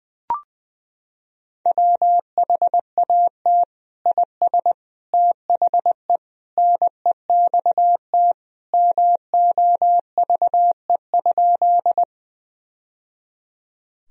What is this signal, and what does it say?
Morse code, heard as a keyed tone, about 20 wpm, 700 Hz. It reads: WHAT IS THE NEXT MOVE?
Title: Sentences from Top 1000 Words - Review Part 1 - 20wpm